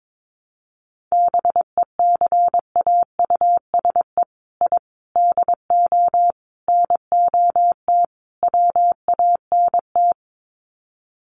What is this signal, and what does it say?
Morse code, heard as a keyed tone, about 22 wpm, 700 Hz. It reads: BECAUSE I DO NOT WANT
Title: Sets of 5 Words Encore 22wpm